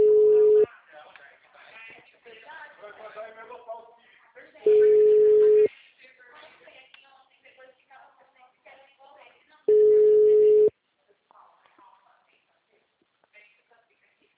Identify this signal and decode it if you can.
Ringback tone (Europe, 425 Hz), with speech.
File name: out- -2033- - - .47923